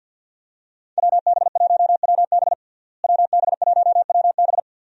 Morse code, heard as a keyed tone, about 52 wpm, 700 Hz. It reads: WB1WB WB1WB
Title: USA Call Signs - 5 Characters - Repeated 2x - 50wpm